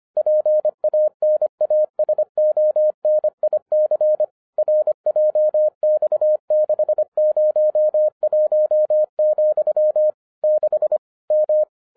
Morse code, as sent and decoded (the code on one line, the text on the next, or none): .--. .- -. .- ... --- -. .. -.-. / .-. .--- -..- -.... ----- .---- --..-- / -.... / --
PANASONIC RJX601, 6 M